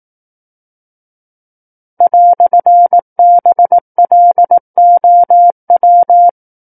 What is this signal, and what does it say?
Morse code, heard as a keyed tone, about 18 wpm, 700 Hz. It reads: "BLOW
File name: 18 WPM